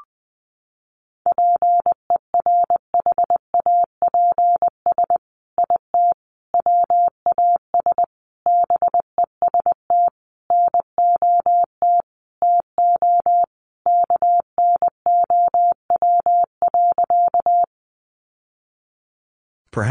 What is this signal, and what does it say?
Morse code, heard as a keyed tone, about 20 wpm, 700 Hz. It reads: PERHAPS IT WAS BEST NOT TO KNOW.
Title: Sentences from Top 600 Words 20wpm